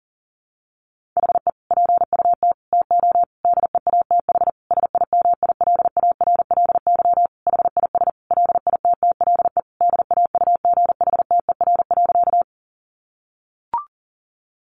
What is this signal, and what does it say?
Morse code, heard as a keyed tone, about 37 wpm, 700 Hz. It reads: HE PUT TO DEATH SIMILARLY HIS LITTLE DAUGHTER.
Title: Sentences from Top 1500 Words 40wpm